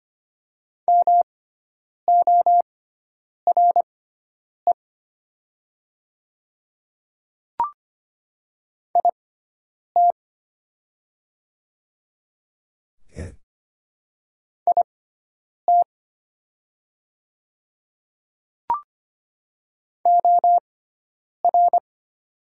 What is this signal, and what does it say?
Morse code, heard as a keyed tone, about 25 wpm, spaced as for 9 wpm, 700 Hz. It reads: MORE IT IT OR
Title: 6x Character Spacing - Top 100 Words - 25wpm